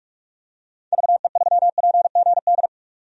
Morse code, heard as a keyed tone, about 46 wpm, 700 Hz. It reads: VE3PGD
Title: DXCC-100 Call Signs - 6 Characters - 45wpm